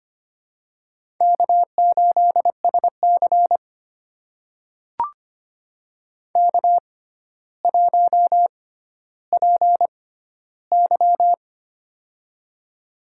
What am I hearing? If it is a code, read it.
Morse code, heard as a keyed tone, about 25 wpm, spaced as for 13 wpm, 700 Hz. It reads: K8SC K1PY
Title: USA Call Signs - 4 Characters - 6x Character Spacing - 25wpm